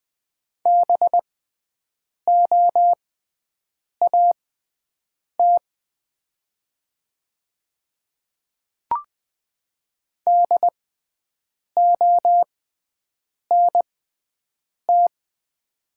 Morse code, heard as a keyed tone, about 20 wpm, spaced as for 7 wpm, 700 Hz. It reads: BOAT DONT
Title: 6x Character Spacing - Top 500 Words - 20wpm